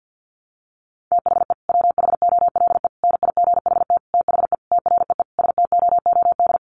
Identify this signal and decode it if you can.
Morse code, heard as a keyed tone, about 50 wpm, 700 Hz. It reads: THE WHOLE NIGHT THE TREE STOOD
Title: Sentences from Top 500 Words 50wpm